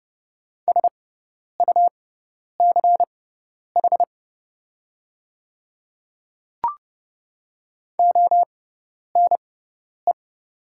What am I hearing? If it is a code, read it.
Morse code, heard as a keyed tone, about 30 wpm, spaced as for 10 wpm, 690 Hz. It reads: SUCH ONE
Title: 6x Character Spacing - Top 200 Words - 30wpm